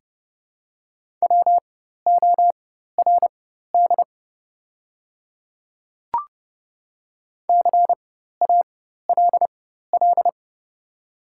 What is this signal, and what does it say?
Morse code, heard as a keyed tone, about 30 wpm, spaced as for 14 wpm, 700 Hz. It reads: WORD CALL
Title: 4x Character Spacing - Top 100 Words - 30wpm